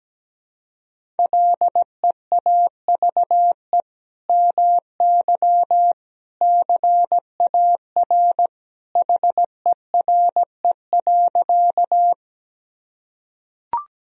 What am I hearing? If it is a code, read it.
Morse code, heard as a keyed tone, about 17 wpm, 700 Hz. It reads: LEAVE MY CAR HERE.
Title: Sentences from Top 1000 Words - Review Part 1 - 17wpm